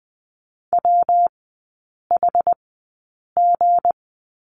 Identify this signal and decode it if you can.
Morse code, heard as a keyed tone, about 20 wpm, spaced as for 8 wpm, 700 Hz. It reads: WHG
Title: 178 Lesson 29e - Slash - Three Characters 20wpm